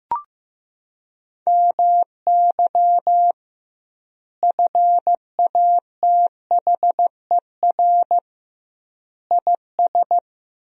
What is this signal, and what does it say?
Morse code, heard as a keyed tone, about 16 wpm, 700 Hz. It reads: MY FATHER IS
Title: 2x Word-Spacing - Sentences from Top 300 Words - 15wpm